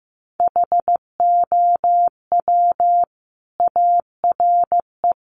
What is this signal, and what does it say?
Morse code, heard as a keyed tone, about 15 wpm, 700 Hz. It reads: HOW ARE
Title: Sentences from Top 1200 Words 15wpm